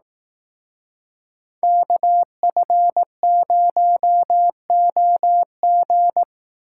Morse code, heard as a keyed tone, about 18 wpm, 700 Hz. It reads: KF0OG
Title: Speed Racing - USA 2x2 Call Signs 15wpm